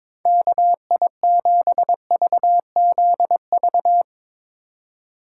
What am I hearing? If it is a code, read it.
Morse code, heard as a keyed tone, about 22 wpm, 700 Hz. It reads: KI7VZV
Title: USA Call Signs - 2x Character Spacing - 22wpm